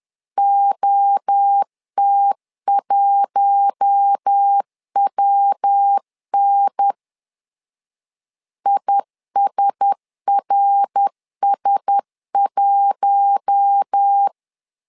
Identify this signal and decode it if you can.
Morse code, heard as a keyed tone, about 10 wpm, 800 Hz. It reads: OT1WN ISRS1
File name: LEZIONI / 4 Q1S 10wpm 2133